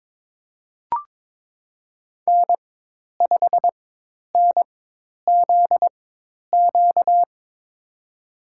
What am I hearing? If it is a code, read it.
Morse code, heard as a keyed tone, about 22 wpm, spaced as for 10 wpm, 700 Hz. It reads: N5NZQ
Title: USA Call Signs - 5 Characters - 4x Character Spacing - 22wpm